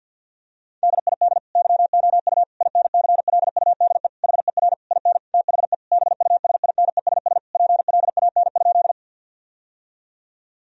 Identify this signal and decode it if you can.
Morse code, heard as a keyed tone, about 50 wpm, 700 Hz. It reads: DID YOU INCLUDE HER IN THE BUSINESS PLAN?
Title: Sentences from Top 1000 Words - Review Part 3 - 50wpm